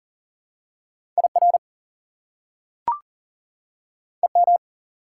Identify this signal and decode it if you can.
Morse code, heard as a keyed tone, about 38 wpm, 700 Hz. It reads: IR EM
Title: Prefixes and Suffixes 40wpm